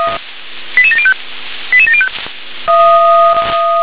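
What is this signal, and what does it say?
Five-tone selective call (ZVEI), heard as quick runs of five tones, tones of about 80 ms. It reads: 88705 88705
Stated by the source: Five-tone ZVEI selective call 88765, received on SDR.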